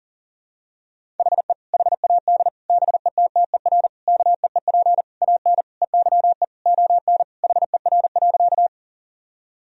Morse code, heard as a keyed tone, about 37 wpm, 700 Hz. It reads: HE HAD BETTER KEEP AN EYE ON HER.
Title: Sentences from Top 1000 Words - Review Part 1 - 40wpm